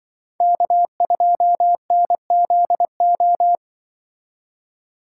Morse code, heard as a keyed tone, about 25 wpm, 700 Hz. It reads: K2NZO
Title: Speed Racing - USA Call Signs 20wpm